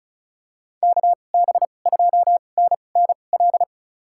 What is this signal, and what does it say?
Morse code, heard as a keyed tone, about 36 wpm, spaced as for 25 wpm, 700 Hz. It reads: KB2NNL
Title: USA Call Signs - 6 Characters - 2x Character Spacing - 35wpm